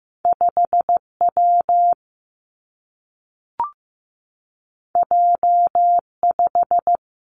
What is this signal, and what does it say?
Morse code, heard as a keyed tone, about 15 wpm, 700 Hz. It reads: 5 W J 5
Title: Two Challenging Letter-Number Combinations 15wpm